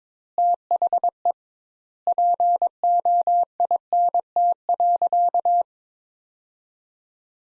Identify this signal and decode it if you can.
Morse code, heard as a keyed tone, about 22 wpm, 700 Hz. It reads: THE POINT.
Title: Sentences Top 200 Words - 22wpm